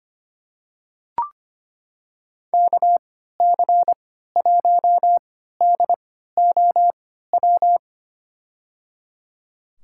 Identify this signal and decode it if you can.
Morse code, heard as a keyed tone, about 25 wpm, spaced as for 14 wpm, 700 Hz. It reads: KC1DOW
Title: USA Call Signs - 6 Characters - 3x Character Spacing - 25wpm